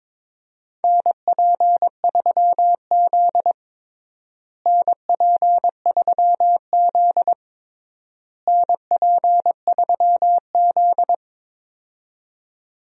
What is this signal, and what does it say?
Morse code, heard as a keyed tone, about 22 wpm, 700 Hz. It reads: NP3Z NP3Z NP3Z
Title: USA Call Signs - 4 Characters - Repeated 3x - 22wpm